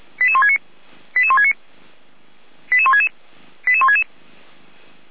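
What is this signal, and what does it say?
Five-tone selective call (ZVEI), heard as quick runs of five tones, tones of about 70 ms. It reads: 88169 88169 88166 88166
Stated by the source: ZVEI five-tone call 88247 received on SDR, logged by the Feuerwehr Orlamünde (f85360) server.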